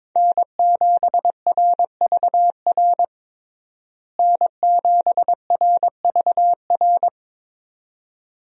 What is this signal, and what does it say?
Morse code, heard as a keyed tone, about 22 wpm, 700 Hz. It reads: N7RVR N7RVR
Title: USA Call Signs - 5 Characters - Repeated 2x - 22wpm